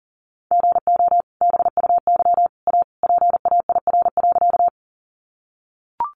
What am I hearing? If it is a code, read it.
Morse code, heard as a keyed tone, about 40 wpm, 700 Hz. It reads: GO BUY A PAIR.
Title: Sentences from Top 1200 Words 40wpm